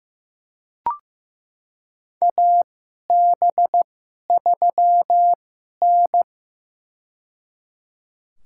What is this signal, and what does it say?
Morse code, heard as a keyed tone, about 15 wpm, spaced as for 11 wpm, 700 Hz. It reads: AB3N
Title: USA Call Signs - 4 Characters - 2x Character Spacing - 15wpm